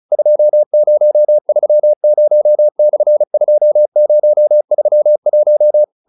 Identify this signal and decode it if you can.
Morse code, heard as a keyed tone, about 34 wpm, 600 Hz. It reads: 2030/2031